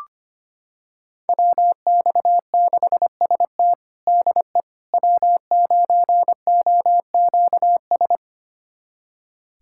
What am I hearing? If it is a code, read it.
Morse code, heard as a keyed tone, about 25 wpm, 700 Hz. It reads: WX6ST DE W9OQS
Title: USA Callsign DE Callsign - 5 Characters - 25wpm